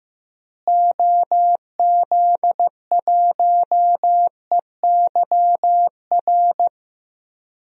Morse code, heard as a keyed tone, about 15 wpm, 700 Hz. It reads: OZ1EYR
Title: DXCC-100 Call Signs - 6 Characters - Repeated 2x - 15wpm